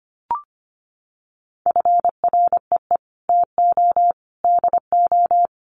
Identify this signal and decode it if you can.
Morse code, heard as a keyed tone, about 25 wpm, 700 Hz. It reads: FREE TO DO
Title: Sentences from Top 500 Words 25wpm